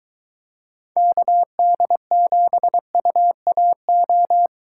Morse code, heard as a keyed tone, about 22 wpm, 700 Hz. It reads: KD7UAO